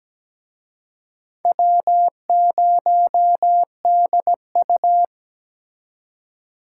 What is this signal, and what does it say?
Morse code, heard as a keyed tone, about 20 wpm, 700 Hz. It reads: W0DU
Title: Speed Racing - USA 1x2 Call Signs 17wpm